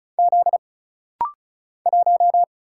Morse code, heard as a keyed tone, about 36 wpm, 700 Hz. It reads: Z 1
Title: Single Letter-Number - Rapid-Fire 35wpm